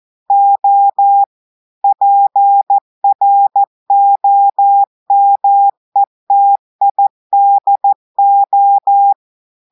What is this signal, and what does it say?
Morse code, heard as a keyed tone, about 14 wpm, 800 Hz. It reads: O PROMETIDO
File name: audios / portugais / portugais texte en clair 1 vitesse 14 mots minutes